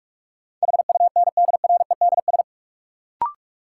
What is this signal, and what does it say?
Morse code, heard as a keyed tone, about 46 wpm, 700 Hz. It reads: HUNDREDS
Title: Sets of 4 Words - New Word Drill - 45wpm